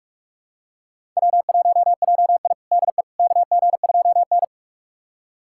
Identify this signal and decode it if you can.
Morse code, heard as a keyed tone, about 42 wpm, 700 Hz. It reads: W1JI DE KG2N
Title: USA Callsign DE Callsign - 4 Characters - 45wpm